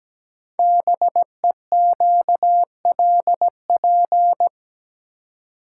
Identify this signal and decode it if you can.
Morse code, heard as a keyed tone, about 17 wpm, 700 Hz. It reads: BEQLP
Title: Code Groups - 5 Letters - 17wpm